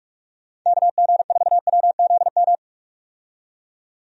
Morse code, heard as a keyed tone, about 46 wpm, 700 Hz. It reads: KG4WZM
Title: USA Call Signs - 6 Characters - 45wpm